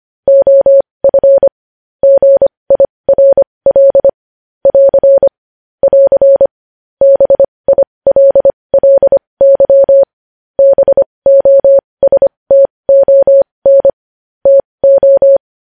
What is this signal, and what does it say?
Morse code, heard as a keyed tone, about 25 wpm, spaced as for 20 wpm, 550 Hz. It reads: OF GIRL <AR> <AR> BILLY BOSTON TO
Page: news headlines in morse code at 20 wpm